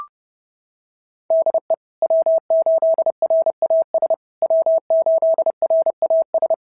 Morse code, heard as a keyed tone, about 28 wpm, 650 Hz. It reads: DE W8RAS W8RAS
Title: SOTA QSO Practice 30wpm